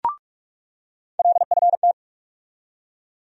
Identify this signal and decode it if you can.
Morse code, heard as a keyed tone, about 46 wpm, 700 Hz. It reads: LFT
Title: USA Call Sign Postfixes - 3 Letters 45wpm